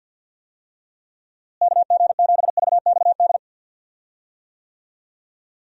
Morse code, heard as a keyed tone, about 51 wpm, 700 Hz. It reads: KG7VXD
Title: USA Call Signs - 6 Characters - Repeated 3x - 50wpm